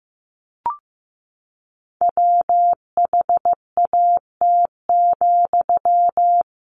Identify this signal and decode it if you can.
Morse code, heard as a keyed tone, about 14 wpm, 700 Hz. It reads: WHAT,
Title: Sentences from Top 600 Words 15wpm